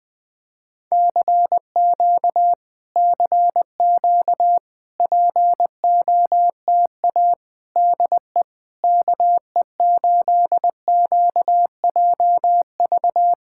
Morse code, heard as a keyed tone, about 20 wpm, 700 Hz. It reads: CQ CQ POTA DE KE8QJV
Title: POTA QSO Practice - 20wpm